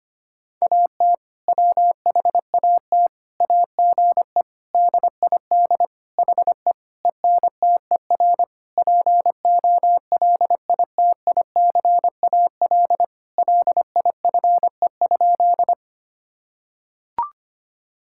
Morse code, heard as a keyed tone, about 25 wpm, 700 Hz. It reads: AT WHAT AGE DID HE ENTER POLITICAL LIFE?
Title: Sentences from Top 1000 Words - Review Part 5 - 25wpm